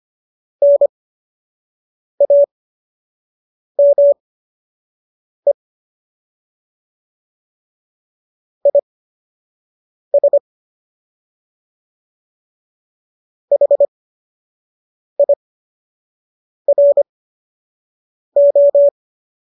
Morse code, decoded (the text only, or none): NAME IS HIRO